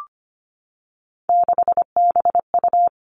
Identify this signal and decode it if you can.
Morse code, heard as a keyed tone, about 25 wpm, 700 Hz. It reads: 6BU
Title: Three Challenging Letter-Number Combinations 25wpm